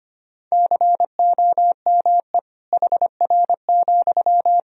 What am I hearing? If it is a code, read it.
Morse code, heard as a keyed tone, about 25 wpm, 700 Hz. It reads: COME HR,
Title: Sentences from Top 300 Words and Abbreviations 25wpm